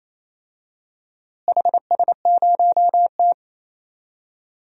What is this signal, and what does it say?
Morse code, heard as a keyed tone, about 28 wpm, 700 Hz. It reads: HS0T
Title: DXCC-100 Call Signs - 4 Characters - Repeated 3x - 28wpm